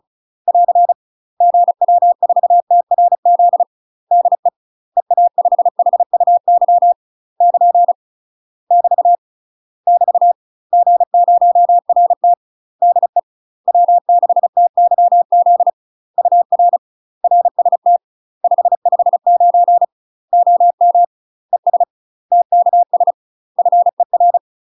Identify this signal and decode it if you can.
Morse code, heard as a keyed tone, about 36 wpm, 700 Hz. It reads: <AR> GW4TRZ DE EA5HUY <KN> <BT> <BT> G0RT DE W6TYZ UR RST 559 OM ES TKS FER
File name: UK MASTER QSOs TRAINING / 35WPM / UK MASTER QSOs 0026